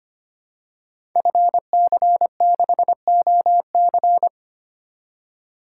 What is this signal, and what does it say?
Morse code, heard as a keyed tone, about 25 wpm, 700 Hz. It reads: FC6OC
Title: Code Groups - 5 Letter:Numbers - 25wpm